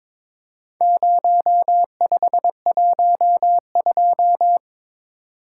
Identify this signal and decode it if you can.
Morse code, heard as a keyed tone, about 22 wpm, 700 Hz. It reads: 0512
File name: Four Letter-Number Combinations - 22wpm